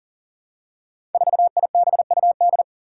Morse code, heard as a keyed tone, about 37 wpm, 700 Hz. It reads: 4IBUD
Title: Five Challenging Letter-Number Combinations 40wpm